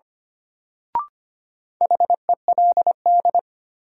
Morse code, heard as a keyed tone, about 25 wpm, 700 Hz. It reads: HELD
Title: Sentences from Top 1600 Words - New Word Drill - 25wpm